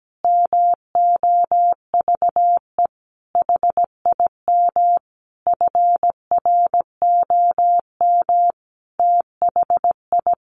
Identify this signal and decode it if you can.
Morse code, heard as a keyed tone, about 17 wpm, 700 Hz. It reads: MOVE HIM FROM THI
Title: Sentences from Top 1300 Words 17wpm